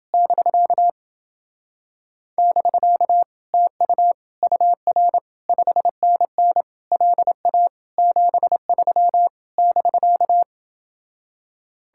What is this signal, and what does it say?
Morse code, heard as a keyed tone, about 29 wpm, 700 Hz. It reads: <BK> <BK> TU UR 5NN LA 73 <BK>